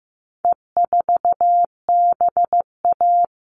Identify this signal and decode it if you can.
Morse code, heard as a keyed tone, about 15 wpm, 700 Hz. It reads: E4BA
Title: All Cut Numbers - 4-Digit 15wpm